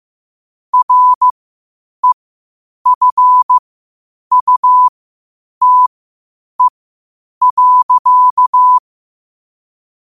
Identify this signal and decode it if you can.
Morse code, heard as a keyed tone, about 15 wpm, spaced as for 8 wpm, 1000 Hz. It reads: REFUTE.
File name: qotdfiles / 8wpm